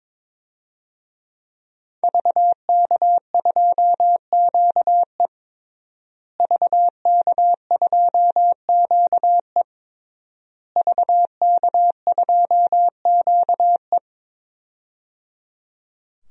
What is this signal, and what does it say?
Morse code, heard as a keyed tone, about 21 wpm, 700 Hz. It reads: VK2QE VK2QE VK2QE
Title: DXCC-100 Call Signs - 5 Characters - Repeated 3x - 22wpm